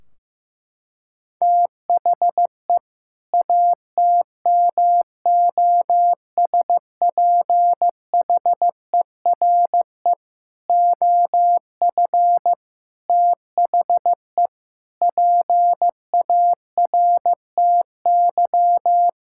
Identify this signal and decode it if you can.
Morse code, heard as a keyed tone, about 15 wpm, 700 Hz. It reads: THE ATMOSPHERE OF THE PARTY
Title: Sentences from Top 1600 Words 15wpm